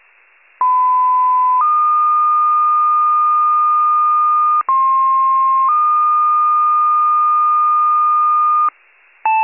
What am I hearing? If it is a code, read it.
Two-tone page, heard as a one-second tone, then a three-second tone, about 1010 then 1220 Hz, about 1060 then 1220 Hz. It is heard from 0.6 to 8.7 s.
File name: pager / 94 RESCUE / wcalert